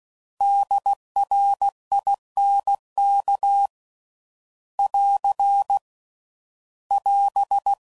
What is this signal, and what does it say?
Morse code, heard as a keyed tone, about 16 wpm, 800 Hz. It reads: DRINK <AR> <AS>
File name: Prefixes / By First Letter / Ds